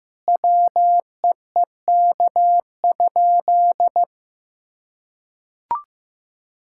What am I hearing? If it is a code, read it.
Morse code, heard as a keyed tone, about 16 wpm, 700 Hz. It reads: WEEK?
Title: Sentences from Top 1000 Words - Review Part 5 - 15wpm